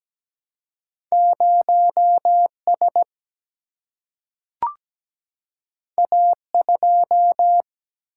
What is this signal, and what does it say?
Morse code, heard as a keyed tone, about 17 wpm, 700 Hz. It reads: 0S A2
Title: Single-Letter Single-Number 17wpm